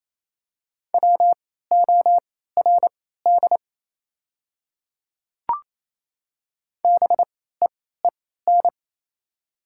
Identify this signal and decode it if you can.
Morse code, heard as a keyed tone, about 27 wpm, spaced as for 16 wpm, 690 Hz. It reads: WORD BEEN
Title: 3x Character Spacing - Top 200 Words - 28wpm